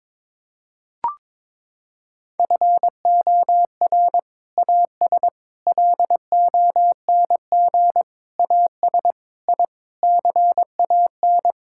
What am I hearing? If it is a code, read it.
Morse code, heard as a keyed tone, about 22 wpm, 700 Hz. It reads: FOR AS LONG AS I CAN
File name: practice / 22wpm / Sets of 8 Words - 22wpm